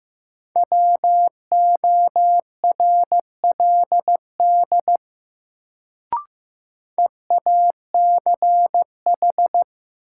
Morse code, heard as a keyed tone, about 15 wpm, 700 Hz. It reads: WORLD EACH
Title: Sentences from Top 200 Words - New Word Drill - 15wpm